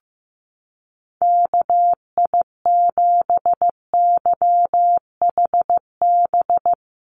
Morse code, heard as a keyed tone, about 15 wpm, 700 Hz. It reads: KI7YHB
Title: USA Call Signs Repeated 3x 15wpm